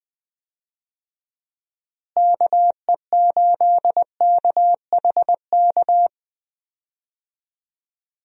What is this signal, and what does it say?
Morse code, heard as a keyed tone, about 20 wpm, 700 Hz. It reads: KE8KHK
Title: USA Call Signs - 6 Characters - Repeated 3x - 20wpm